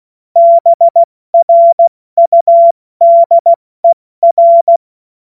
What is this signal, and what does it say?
Morse code, heard as a keyed tone, about 16 wpm, 680 Hz. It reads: BRUDER
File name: Morsekurs / Witze / Witz 06. Juli 2021, 70 F80